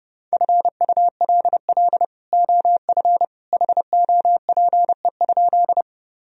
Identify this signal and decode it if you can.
Morse code, heard as a keyed tone, about 30 wpm, 700 Hz. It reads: FULL OF HOPE?
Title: Sentences from Top 1000 Words - Review Part 3 - 30wpm